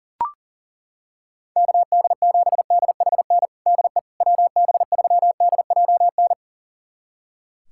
Morse code, heard as a keyed tone, about 40 wpm, 700 Hz. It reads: KD7DHN DE WB3DJN
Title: USA Callsign DE Callsign 40wpm